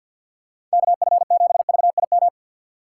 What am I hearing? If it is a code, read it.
Morse code, heard as a keyed tone, about 53 wpm, 700 Hz. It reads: KF7VIM
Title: USA Call Signs - 6 Characters - 3x Character Spacing - 50wpm